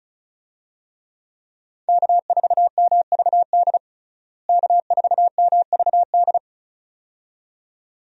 Morse code, heard as a keyed tone, about 35 wpm, 700 Hz. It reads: K4MVD K4MVD
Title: USA Call Signs - 5 Characters - Repeated 2x - 35wpm